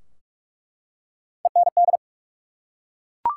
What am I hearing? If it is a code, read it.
Morse code, heard as a keyed tone, about 43 wpm, 700 Hz. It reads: END
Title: Three Letter Words - 45wpm